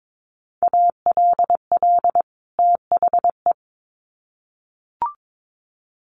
Morse code, heard as a keyed tone, about 22 wpm, 700 Hz. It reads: ALL THE
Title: Sets of 2 Words 22wpm